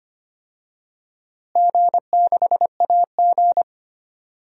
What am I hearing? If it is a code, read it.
Morse code, heard as a keyed tone, about 25 wpm, 700 Hz. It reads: G6AG
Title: DXCC-100 Call Signs - 4 Characters - Repeated 3x - 25wpm